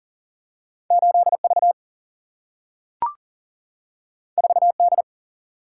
Morse code, heard as a keyed tone, about 41 wpm, 700 Hz. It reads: 8V 4D
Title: Single-Letter Single-Number 40wpm